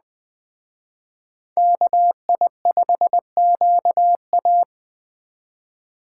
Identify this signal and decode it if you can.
Morse code, heard as a keyed tone, about 20 wpm, 700 Hz. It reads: KI5QA
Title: Speed Racing - USA Call Signs 17wpm